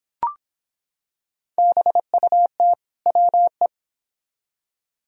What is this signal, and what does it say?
Morse code, heard as a keyed tone, about 25 wpm, 700 Hz. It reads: BUT WE